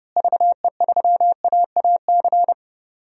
Morse code, heard as a keyed tone, about 30 wpm, 700 Hz. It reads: VE3AAC
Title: DXCC-100 Call Signs - 6 Characters - Repeated 3x - 30wpm